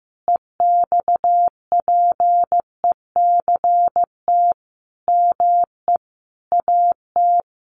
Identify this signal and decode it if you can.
Morse code, heard as a keyed tone, about 15 wpm, 700 Hz. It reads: EXPECT ME AT
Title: Sentences from Top 1100 Words 15wpm